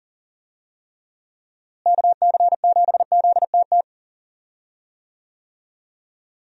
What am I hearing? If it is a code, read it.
Morse code, heard as a keyed tone, about 37 wpm, 700 Hz. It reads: KC7ZTT
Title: USA Call Signs - 6 Characters - Repeated 3x - 40wpm